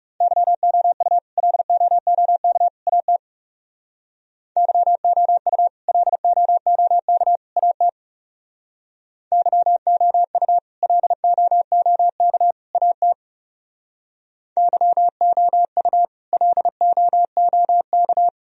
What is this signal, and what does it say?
Morse code, heard as a keyed tone, about 40 wpm, 700 Hz. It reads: YOU LOOK AT YOU LOOK AT YOU LOOK AT YOU LOOK